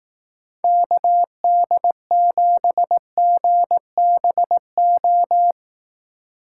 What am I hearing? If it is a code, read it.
Morse code, heard as a keyed tone, about 18 wpm, 700 Hz. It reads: KD7GBO
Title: Speed Racing - USA 2x3 Call Signs 15wpm